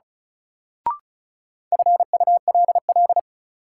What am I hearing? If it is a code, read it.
Morse code, heard as a keyed tone, about 36 wpm, 700 Hz. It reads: FULL
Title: Sentences from Top 500 Words - New Word Drill - 35wpm